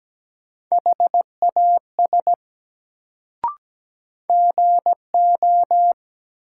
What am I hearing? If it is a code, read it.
Morse code, heard as a keyed tone, about 17 wpm, 700 Hz. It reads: HAS GO
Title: Sentences from Top 100 Words - New Word Drill - 17wpm